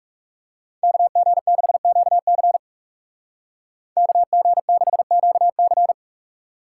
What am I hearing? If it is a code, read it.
Morse code, heard as a keyed tone, about 34 wpm, 700 Hz. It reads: KG6QC KG6QC
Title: Speed Racing - USA 2x2 Call Signs 30wpm